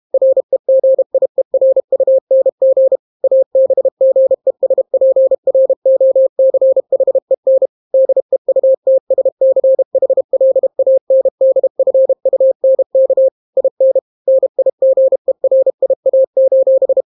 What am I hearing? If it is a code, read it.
Morse code, heard as a keyed tone, about 32 wpm, 530 Hz. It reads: REGIERUNG ABGESPROCHEN DEUTSCHLANDFUNK IN NIGERIA: